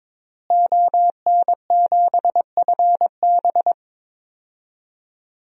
Morse code, heard as a keyed tone, about 23 wpm, 700 Hz. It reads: ON7FB
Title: DXCC-100 Call Signs - 5 Characters - Repeated 3x - 22wpm